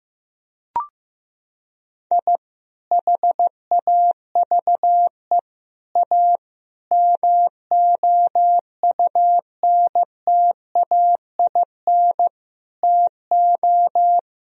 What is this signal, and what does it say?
Morse code, heard as a keyed tone, about 16 wpm, 700 Hz. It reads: I HAVE A MOUNTAIN TO
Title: Sentences from Top 1000 Words - Review Part 1 - 15wpm